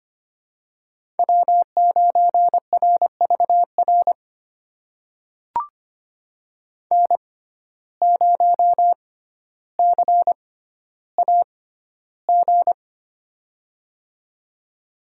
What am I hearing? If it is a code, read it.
Morse code, heard as a keyed tone, about 25 wpm, spaced as for 13 wpm, 700 Hz. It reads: W9RVR N0CAG
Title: USA Call Signs - 6x Character Spacing - 25wpm